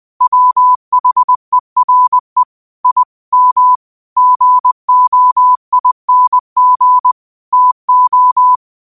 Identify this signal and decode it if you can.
Morse code, heard as a keyed tone, about 20 wpm, 1000 Hz. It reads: WHERE I M GOING TO